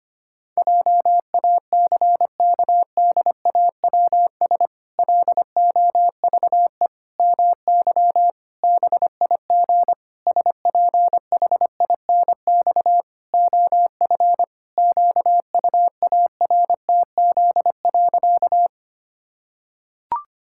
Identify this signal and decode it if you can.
Morse code, heard as a keyed tone, about 25 wpm, 700 Hz. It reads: JACKDAWS LOVE MY BIG SPHINX OF QUARTZ.
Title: Pangram Challenge v2 - 25wpm